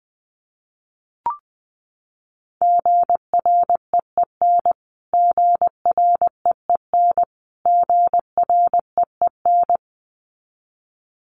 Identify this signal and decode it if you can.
Morse code, heard as a keyed tone, about 19 wpm, 700 Hz. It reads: GREEN GREEN GREEN
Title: Top 500 Words Repeated 20wpm